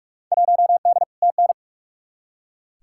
Morse code, heard as a keyed tone, about 44 wpm, 700 Hz. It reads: 1D TN
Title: Field Day Exchanges - 45wpm